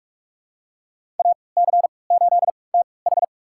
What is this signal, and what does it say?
Morse code, heard as a keyed tone, about 46 wpm, spaced as for 25 wpm, 700 Hz. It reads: AC8TH
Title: USA Call Signs - 5 Characters - 3x Character Spacing - 45wpm